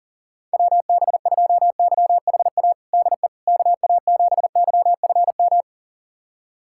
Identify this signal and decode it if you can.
Morse code, heard as a keyed tone, about 40 wpm, 700 Hz. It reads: WB2YHA DE KA7YFM
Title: USA Callsign DE Callsign - 6 Characters - 40wpm